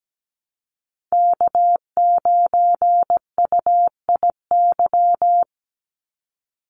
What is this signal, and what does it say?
Morse code, heard as a keyed tone, about 17 wpm, 700 Hz. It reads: K9UIY
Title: Call Signs - 17wpm